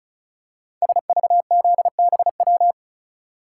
Morse code, heard as a keyed tone, about 36 wpm, 690 Hz. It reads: SVZBW
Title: Code Groups - 5 Letters - 35wpm